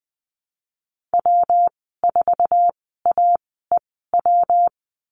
Morse code, heard as a keyed tone, about 20 wpm, spaced as for 15 wpm, 700 Hz. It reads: W4AEW
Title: 167 Lesson 27g - Number 7 - Call Signs 20wpm